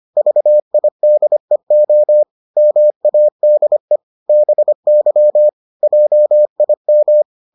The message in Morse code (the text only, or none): VIDEO MADE BY JIM